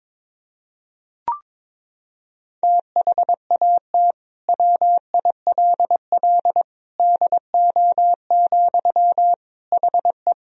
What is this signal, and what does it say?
Morse code, heard as a keyed tone, about 22 wpm, 700 Hz. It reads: THAT WILL DO, HE
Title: Sentences from Top 100 Words and Abbreviations 22wpm